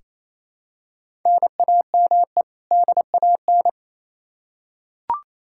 Morse code, heard as a keyed tone, about 27 wpm, 700 Hz. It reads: NAME DAN
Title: Practice QSO - Exchange 1 - Beginning - 28wpm